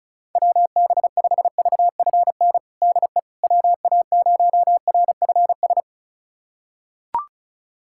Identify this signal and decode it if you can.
Morse code, heard as a keyed tone, about 36 wpm, 700 Hz. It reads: WB5VFN DE WA0RFS
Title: USA Callsign DE Callsign - 6 Characters - 35wpm